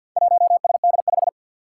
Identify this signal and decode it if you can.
Morse code, heard as a keyed tone, about 48 wpm, 700 Hz. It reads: 1SD5
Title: Four Challenging Letter-Number Combinations 50wpm